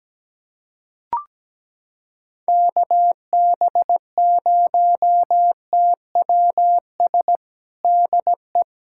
Morse code, heard as a keyed tone, about 17 wpm, 700 Hz. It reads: KB0TWS DE
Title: USA Callsign DE Callsign - 6 Characters - 17wpm